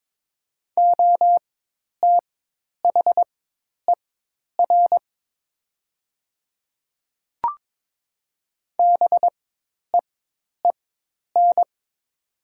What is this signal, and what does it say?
Morse code, heard as a keyed tone, about 22 wpm, spaced as for 10 wpm, 700 Hz. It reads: OTHER BEEN
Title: 4x Character Spacing - Top 100 Words - 22wpm